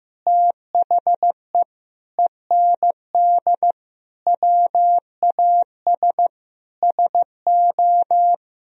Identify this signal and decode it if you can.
Morse code, heard as a keyed tone, about 15 wpm, 700 Hz. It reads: THE END WAS SO
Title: Sentences from Top 300 Words and Abbreviations 15wpm